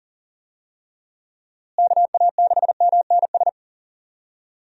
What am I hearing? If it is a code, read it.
Morse code, heard as a keyed tone, about 40 wpm, 700 Hz. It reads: KA6MNS
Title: USA Call Signs - 6 Characters - Repeated 3x - 40wpm